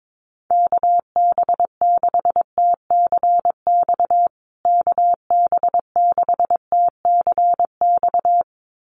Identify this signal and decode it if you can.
Morse code, heard as a keyed tone, about 22 wpm, 700 Hz. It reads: KB6TCX KB6TCX
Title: USA Call Signs Repeated 2x 22wpm